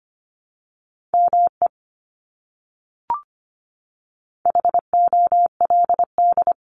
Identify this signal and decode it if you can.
Morse code, heard as a keyed tone, about 25 wpm, 700 Hz. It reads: ME HOLD
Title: Top 500 Words Repeated 25wpm